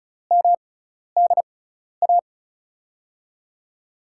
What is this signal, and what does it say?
Morse code, heard as a keyed tone, about 36 wpm, spaced as for 12 wpm, 700 Hz. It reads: MDA
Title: USA Call Signs - 5 Characters - 6x Character Spacing - 35wpm